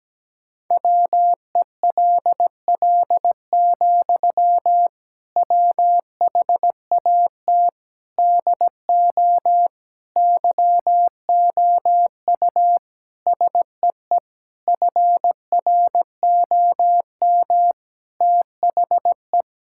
Morse code, heard as a keyed tone, about 17 wpm, 700 Hz. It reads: WELL, WHAT DO YOU SEE FROM THE
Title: Sentences from Top 1600 Words 17wpm